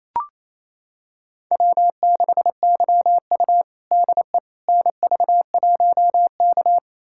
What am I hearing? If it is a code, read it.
Morse code, heard as a keyed tone, about 28 wpm, 700 Hz. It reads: W6YU DE NV1K
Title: USA Callsign DE Callsign - 4 Characters - 28wpm